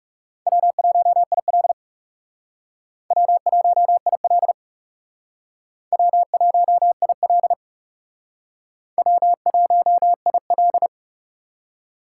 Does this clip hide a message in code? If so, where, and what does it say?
Yes: Morse code from 0.5 to 10.9 s, heard as a keyed tone, about 40 wpm, 700 Hz. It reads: W1IL W1IL W1IL W1IL